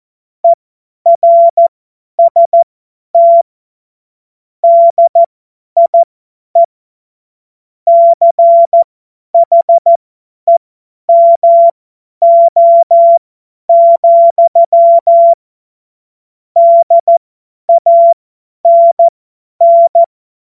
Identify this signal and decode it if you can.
Morse code, heard as a keyed tone, about 14 wpm, spaced as for 10 wpm, 680 Hz. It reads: ERST DIE CHEMO, DANN
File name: Morsekurs / Witze / Witz 19. Oktober 2021, 50 F70